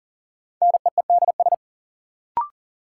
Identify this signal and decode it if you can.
Morse code, heard as a keyed tone, about 39 wpm, 700 Hz. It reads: NEEDS
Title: Sentences from Top 1000 Words - Review Part 3 - New Word Drill - 40wpm